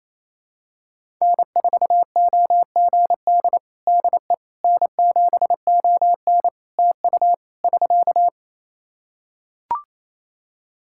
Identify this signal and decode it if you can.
Morse code, heard as a keyed tone, about 27 wpm, 700 Hz. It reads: N4OGD DE N7ON TU <SK>
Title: Practice QSO - Exchange 4 - Ending - 28wpm